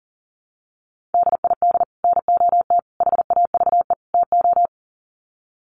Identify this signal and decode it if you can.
Morse code, heard as a keyed tone, about 40 wpm, 700 Hz. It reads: DID NOT HAVE TO
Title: Sets of 4 Words 40wpm